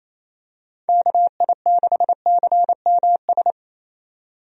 Morse code, heard as a keyed tone, about 28 wpm, 700 Hz. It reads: KI6CMS
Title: USA Call Signs - 6 Characters - 28wpm